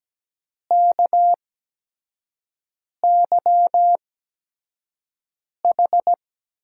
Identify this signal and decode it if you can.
Morse code, heard as a keyed tone, about 17 wpm, spaced as for 5 wpm, 700 Hz. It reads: KYH
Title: USA Call Signs - 6 Characters - 8x Character Spacing - 17wpm